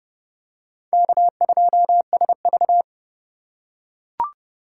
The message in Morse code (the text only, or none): K2SV